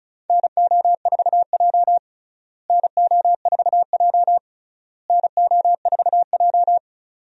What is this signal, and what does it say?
Morse code, heard as a keyed tone, about 36 wpm, 700 Hz. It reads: NO4J NO4J NO4J
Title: USA Call Signs - 4 Characters - Repeated 3x - 35wpm